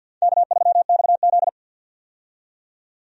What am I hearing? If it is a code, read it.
Morse code, heard as a keyed tone, about 45 wpm, 700 Hz. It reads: K3XZ
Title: USA Call Signs - 4 Characters - 50wpm